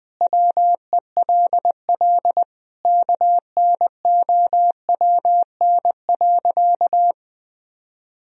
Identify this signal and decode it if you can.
Morse code, heard as a keyed tone, about 21 wpm, 700 Hz. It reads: WELL KNOWN.
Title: Sentences from Top 1000 Words - Review Part 3 - 20wpm